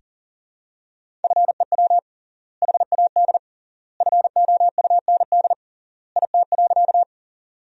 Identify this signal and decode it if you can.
Morse code, heard as a keyed tone, about 37 wpm, 700 Hz. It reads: FEW HAD FOUND IT.
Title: 3x Word-Spacing - Sentences from Top 300 Words - 40wpm